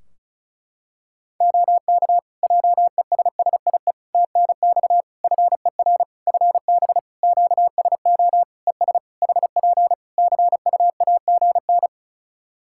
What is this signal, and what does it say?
Morse code, heard as a keyed tone, about 36 wpm, 700 Hz. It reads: OK JESSIE TNX FER FB QSO ES HP CUAGN
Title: Practice QSO - Exchange 4 - Ending - 35wpm